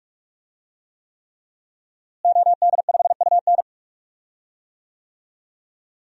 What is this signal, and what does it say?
Morse code, heard as a keyed tone, about 41 wpm, 700 Hz. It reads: OD5UN
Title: DXCC-100 Call Signs - 5 Characters - Repeated 3x - 45wpm